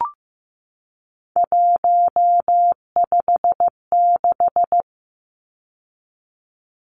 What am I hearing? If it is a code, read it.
Morse code, heard as a keyed tone, about 15 wpm, 700 Hz. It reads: 156
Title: Numbers - 3-Digit 15wpm